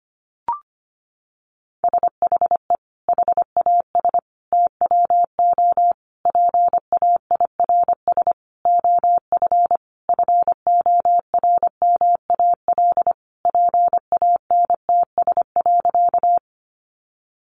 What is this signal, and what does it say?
Morse code, heard as a keyed tone, about 25 wpm, 700 Hz. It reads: SHE HAS TWO PAIRS OF FORMAL PANTS.
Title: Additional Easy Phrases 25wpm